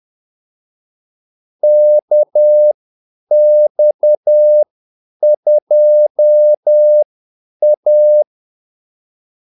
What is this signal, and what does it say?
Morse code, heard as a keyed tone, about 10 wpm, spaced as for 8 wpm, 600 Hz. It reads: KX2A
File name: Callsigns-10-8wpm